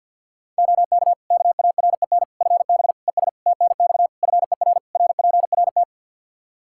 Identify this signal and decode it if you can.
Morse code, heard as a keyed tone, about 53 wpm, 700 Hz. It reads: OK KAREN FB ES TNX FER RPRT
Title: Practice QSO - Exchange 2 - Info - 50wpm